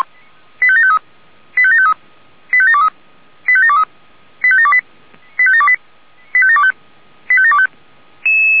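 Five-tone selective call (ZVEI), heard as quick runs of five tones, tones of about 70 ms. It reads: 85753 85753 85723 85723 85728 85728 85725
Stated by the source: ZVEI five-tone selective call 85726, received by SDR.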